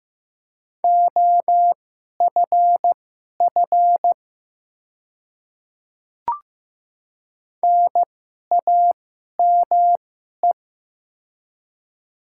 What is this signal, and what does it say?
Morse code, heard as a keyed tone, about 16 wpm, spaced as for 11 wpm, 700 Hz. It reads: OFF NAME
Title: 2x Character Spacing - Top 200 Words - 15wpm